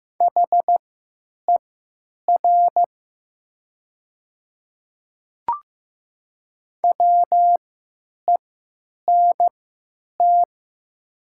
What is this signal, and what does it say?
Morse code, heard as a keyed tone, about 15 wpm, spaced as for 9 wpm, 700 Hz. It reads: HER WENT
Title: 3x Character Spacing - Top 300 Words - 15wpm